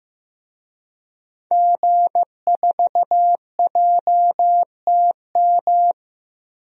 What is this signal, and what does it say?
Morse code, heard as a keyed tone, about 15 wpm, 700 Hz. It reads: G4JTM
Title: DXCC-100 Call Signs - 5 Characters - Repeated 3x - 15wpm